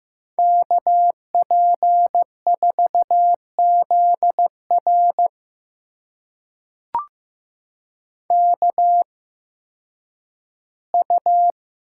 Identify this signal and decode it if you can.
Morse code, heard as a keyed tone, about 15 wpm, spaced as for 10 wpm, 700 Hz. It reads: KP4ZR KU
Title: USA Call Signs - 5 Characters - 8x Character Spacing - 15wpm